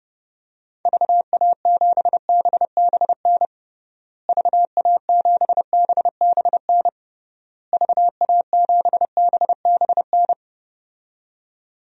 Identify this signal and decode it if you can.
Morse code, heard as a keyed tone, about 32 wpm, 700 Hz. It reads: VA7BBN VA7BBN VA7BBN
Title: DXCC-100 Call Signs - 6 Characters - Repeated 3x - 30wpm